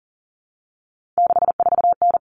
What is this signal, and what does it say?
Morse code, heard as a keyed tone, about 40 wpm, 700 Hz. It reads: 64N
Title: All Cut Numbers - 3-Digit 40wpm